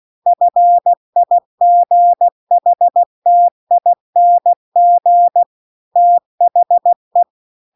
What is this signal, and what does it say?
Morse code, heard as a keyed tone, about 16 wpm, 700 Hz. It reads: FIGHTING THE